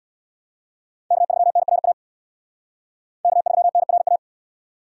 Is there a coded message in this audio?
Yes: Morse code from 1.1 to 4.2 s, heard as a keyed tone, about 63 wpm, 700 Hz. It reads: K3NRA K3NRA